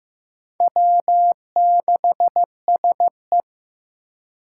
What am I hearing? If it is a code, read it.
Morse code, heard as a keyed tone, about 16 wpm, 700 Hz. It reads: W6SE
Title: USA Call Signs - 4 Characters - 4x Character Spacing - 15wpm